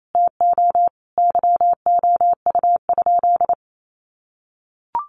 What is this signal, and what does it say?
Morse code, heard as a keyed tone, about 28 wpm, 700 Hz. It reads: TO YOU?
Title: Sentences from Top 1300 Words 28wpm